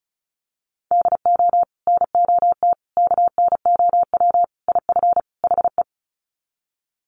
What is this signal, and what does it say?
Morse code, heard as a keyed tone, about 34 wpm, 700 Hz. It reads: DO NOT KNOW IF HE
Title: Sets of 5 Words Encore2 35wpm